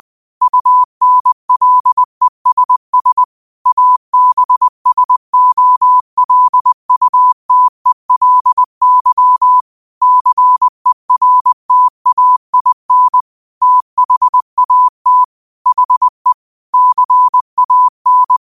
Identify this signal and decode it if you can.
Morse code, heard as a keyed tone, about 20 wpm, 1000 Hz. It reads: UNLESS ABSOLUTELY CERTAIN THAT HE CAN
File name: qotdfiles / 20wpm